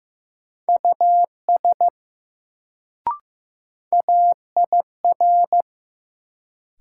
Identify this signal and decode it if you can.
Morse code, heard as a keyed tone, about 15 wpm, 700 Hz. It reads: US AIR
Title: Sets of 2 Words - New Word Drill - 15wpm